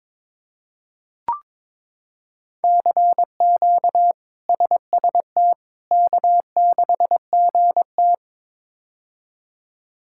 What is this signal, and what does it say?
Morse code, heard as a keyed tone, about 22 wpm, 700 Hz. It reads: CQ SST K6GT
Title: K1USN SST QSO Practice - 22wpm